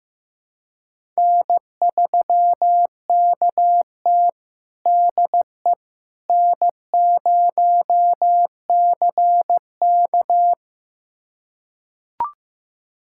Morse code, heard as a keyed tone, about 15 wpm, 700 Hz. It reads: N3KT DE N0CK
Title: USA Callsign DE Callsign - 4 Characters - 15wpm